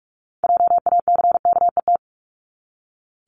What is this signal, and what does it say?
Morse code, heard as a keyed tone, about 44 wpm, 700 Hz. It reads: JACKET
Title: Top 1751-2000 Words 45wpm